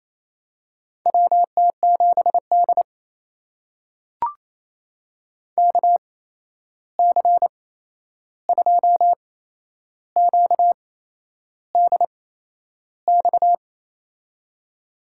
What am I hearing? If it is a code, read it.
Morse code, heard as a keyed tone, about 28 wpm, spaced as for 11 wpm, 700 Hz. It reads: WT7D KC2QDX
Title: USA Call Signs - 8x Character Spacing - 28wpm